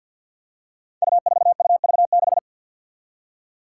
Morse code, heard as a keyed tone, about 48 wpm, 700 Hz. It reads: U4UV6
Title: Five Challenging Letter-Number Combinations 50wpm